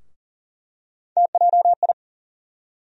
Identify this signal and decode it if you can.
Morse code, heard as a keyed tone, about 40 wpm, 700 Hz. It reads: TJI
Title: USA Call Sign Postfixes - 3 Letters 40wpm